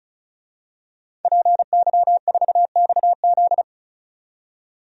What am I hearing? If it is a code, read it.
Morse code, heard as a keyed tone, about 34 wpm, 700 Hz. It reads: PY4XZ
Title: DXCC-100 Call Signs - 5 Characters - 35wpm